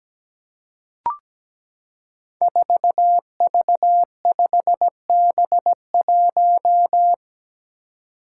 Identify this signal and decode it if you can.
Morse code, heard as a keyed tone, about 17 wpm, 700 Hz. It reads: 4V5B1
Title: Five Challenging Letter-Number Combinations 17wpm